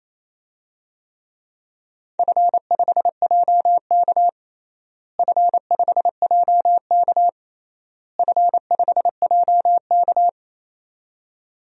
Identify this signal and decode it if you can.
Morse code, heard as a keyed tone, about 28 wpm, 700 Hz. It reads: F5JK F5JK F5JK
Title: DXCC-100 Call Signs - 4 Characters - Repeated 3x - 28wpm